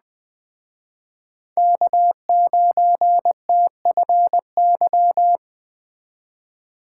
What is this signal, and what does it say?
Morse code, heard as a keyed tone, about 20 wpm, 700 Hz. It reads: K9TFY
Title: Speed Racing - USA 1x3 Call Signs 17wpm